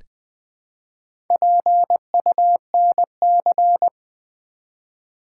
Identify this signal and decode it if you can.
Morse code, heard as a keyed tone, about 20 wpm, 700 Hz. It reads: PUNC
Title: Build A Word - 1 Letter At A Time - 3 to 5-Letter Words - 20wpm